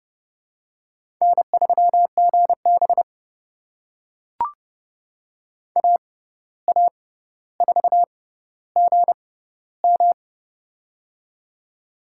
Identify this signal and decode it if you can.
Morse code, heard as a keyed tone, about 32 wpm, spaced as for 15 wpm, 700 Hz. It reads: N3GB AA4GM
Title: USA Call Signs - 6x Character Spacing - 30wpm